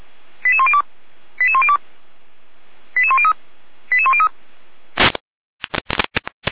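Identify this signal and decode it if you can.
Five-tone selective call (ZVEI), heard as quick runs of five tones, tones of about 70 ms. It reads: 88102 88102 88103 88103
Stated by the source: ZVEI five-tone selective call 89020 received on an SDR.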